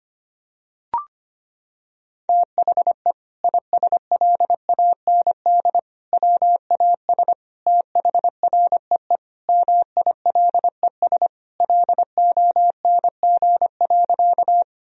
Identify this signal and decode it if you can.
Morse code, heard as a keyed tone, about 25 wpm, 700 Hz. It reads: THE ISLAND WAS THREE MILES LONG.
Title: Sentences from Top 1000 Words - Review Part 3 - 25wpm